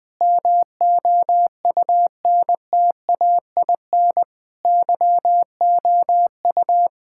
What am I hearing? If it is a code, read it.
Morse code, heard as a keyed tone, about 20 wpm, 700 Hz. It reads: MOUNTAIN YOU
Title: Sentences with 3-Word Phrases - 20wpm